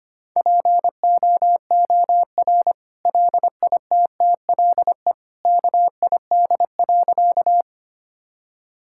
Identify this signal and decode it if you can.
Morse code, heard as a keyed tone, about 25 wpm, 700 Hz. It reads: POOR LITTLE KID.
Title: Sentences from Top 1600 Words 25wpm